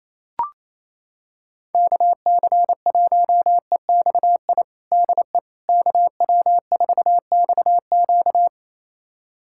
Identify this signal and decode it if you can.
Morse code, heard as a keyed tone, about 28 wpm, 700 Hz. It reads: KC1EXI DE KW4XQ
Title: USA Callsign DE Callsign 28wpm